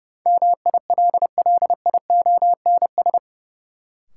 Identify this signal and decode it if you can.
Morse code, heard as a keyed tone, about 30 wpm, 700 Hz. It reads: MILLIONS
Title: Sets of 5 Words - New Word Drill - 30wpm